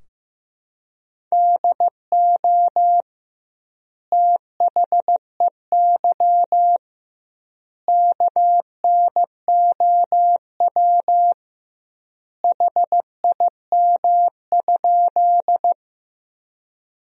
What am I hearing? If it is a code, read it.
Morse code, heard as a keyed tone, about 16 wpm, 700 Hz. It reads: DO THEY KNOW HIM?
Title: 2x Word-Spacing - Sentences from Top 100 Words - 15wpm